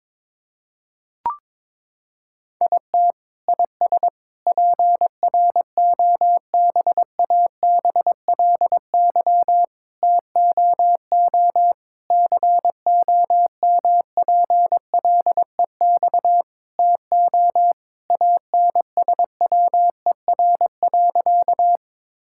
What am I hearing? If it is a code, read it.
Morse code, heard as a keyed tone, about 22 wpm, 700 Hz. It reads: IT IS PROBABLY TOO COMPLEX TO ANSWER.
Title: Sentences from Top 1000 Words - Review Part 6 - 22wpm